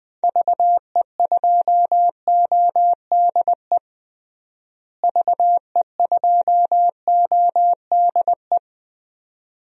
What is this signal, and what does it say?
Morse code, heard as a keyed tone, about 20 wpm, 700 Hz. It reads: VE2ODE VE2ODE
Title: DXCC-100 Call Signs - 6 Characters - Repeated 2x - 20wpm